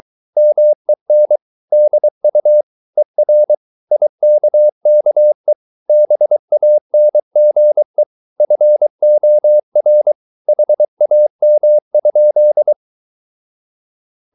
Morse code, heard as a keyed tone, about 23 wpm, 600 Hz. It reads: MEN DU ER IKKE BANGE FOR HAM?
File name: Ferie QRS 22wpm